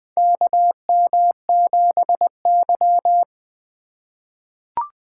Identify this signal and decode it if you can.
Morse code, heard as a keyed tone, about 20 wpm, 700 Hz. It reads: KM7Y
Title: USA Call Signs - 4 Characters - 3x Character Spacing - 20wpm